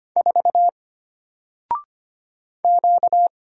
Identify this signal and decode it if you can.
Morse code, heard as a keyed tone, about 25 wpm, 700 Hz. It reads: 4 Q
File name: Single-Letter-Number-25wpm